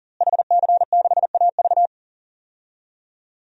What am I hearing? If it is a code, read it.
Morse code, heard as a keyed tone, about 40 wpm, 700 Hz. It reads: HC6AV
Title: DXCC-100 Call Signs - 5 Characters - Repeated 2x - 40wpm